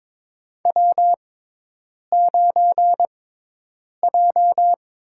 Morse code, heard as a keyed tone, about 23 wpm, spaced as for 8 wpm, 700 Hz. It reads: W9J
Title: USA Call Signs - 4 Characters - 6x Character Spacing - 22wpm